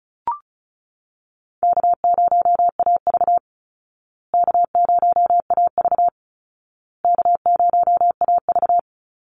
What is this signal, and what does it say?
Morse code, heard as a keyed tone, about 34 wpm, 700 Hz. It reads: K0AV K0AV K0AV
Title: Easy Call Signs Repeated 3x - 35wpm